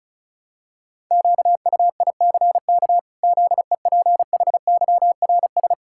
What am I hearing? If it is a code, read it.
Morse code, heard as a keyed tone, about 36 wpm, 700 Hz. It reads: QUICK ZEPHYRS
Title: Pangram Challenge v2 - 35wpm